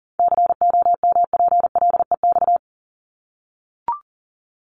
Morse code, heard as a keyed tone, about 40 wpm, 700 Hz. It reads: COMPLEX
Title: Top 751-1000 Words 40wpm